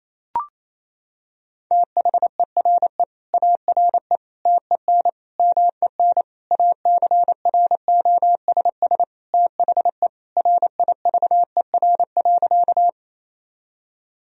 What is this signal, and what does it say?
Morse code, heard as a keyed tone, about 28 wpm, 700 Hz. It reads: THERE ARE TEN MEN ACROSS THE RIVER.
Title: Sentences from Top 1000 Words - Review Part 2 - 28wpm